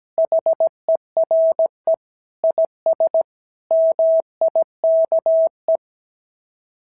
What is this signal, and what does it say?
Morse code, heard as a keyed tone, about 17 wpm, 690 Hz. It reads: HERE IS MIKE
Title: SOTA QSO Practice 17wpm